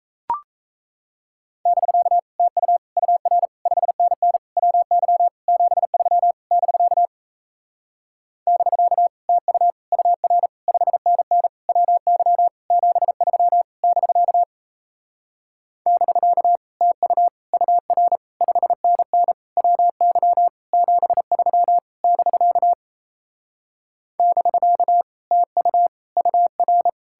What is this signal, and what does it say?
Morse code, heard as a keyed tone, about 37 wpm, 700 Hz. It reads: <BK> TU UR 5NN WY 73 <BK> <BK> TU UR 5NN WY 73 <BK> <BK> TU UR 5NN WY 73 <BK> <BK> TU UR